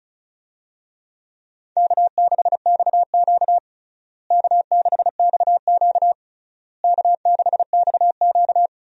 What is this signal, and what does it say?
Morse code, heard as a keyed tone, about 36 wpm, 700 Hz. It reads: K6XQ K6XQ K6XQ
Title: USA Call Signs - 4 Characters - Repeated 3x - 35wpm